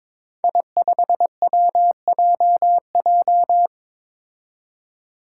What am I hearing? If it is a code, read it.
Morse code, heard as a keyed tone, about 22 wpm, 700 Hz. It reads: I5WJJ
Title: Five Challenging Letter-Number Combinations 22wpm